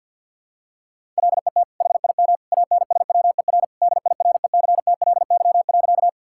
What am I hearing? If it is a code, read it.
Morse code, heard as a keyed tone, about 52 wpm, 690 Hz. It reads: LET HIM ANSWER DIRECTLY.
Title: Sentences from Top 1000 Words - Review Part 5 - 50wpm